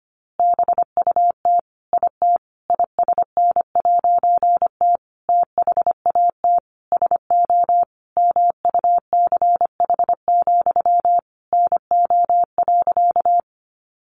Morse code, heard as a keyed tone, about 25 wpm, 700 Hz. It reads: BUT IT ISN'T THAT SO MUCH, NO.
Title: A Sound Of Thunder 25wpm